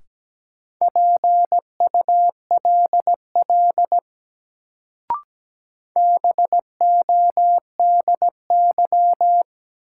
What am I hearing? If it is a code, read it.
Morse code, heard as a keyed tone, about 17 wpm, 700 Hz. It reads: PULL BODY
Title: Sentences from Top 400 Words - New Word Drill - 17wpm